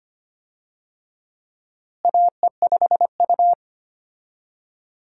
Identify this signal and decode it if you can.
Morse code, heard as a keyed tone, about 25 wpm, 700 Hz. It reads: AE5U
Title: USA Call Signs - 4 Characters - Repeated 2x - 25wpm